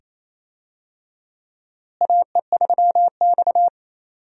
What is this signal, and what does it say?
Morse code, heard as a keyed tone, about 28 wpm, 700 Hz. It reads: AE3X
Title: USA Call Signs - 4 Characters - Repeated 2x - 28wpm